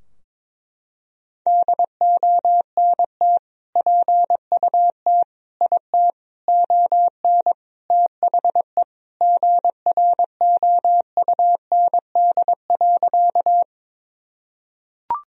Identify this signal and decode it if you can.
Morse code, heard as a keyed tone, about 22 wpm, 700 Hz. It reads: DONT PUT IT ON THE GROUND.
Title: Sentences from Top 1000 Words - Review Part 2 - 22wpm